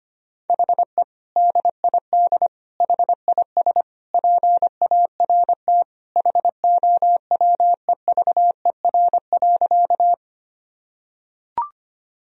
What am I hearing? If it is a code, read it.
Morse code, heard as a keyed tone, about 25 wpm, 700 Hz. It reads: HE DID HIS PART HOWEVER.
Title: Sentences from Top 1000 Words - Review Part 1 - 25wpm